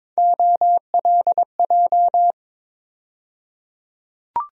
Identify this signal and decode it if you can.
Morse code, heard as a keyed tone, about 22 wpm, 700 Hz. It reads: OLJ
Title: USA Call Sign Postfixes - 3 Letters 22wpm